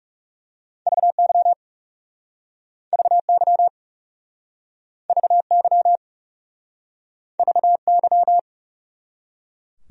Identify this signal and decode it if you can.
Morse code, heard as a keyed tone, about 40 wpm, 700 Hz. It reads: VY VY VY VY